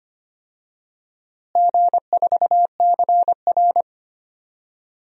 Morse code, heard as a keyed tone, about 25 wpm, 700 Hz. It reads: G4CR
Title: DXCC-100 Call Signs - 4 Characters - Repeated 3x - 25wpm